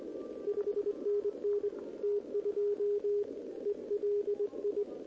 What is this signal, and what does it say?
Morse code, heard as a keyed tone, about 22 wpm, 410 Hz, with radio noise. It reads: 5NN T2 ELI
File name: OK QRP 2026 / OKQRP2026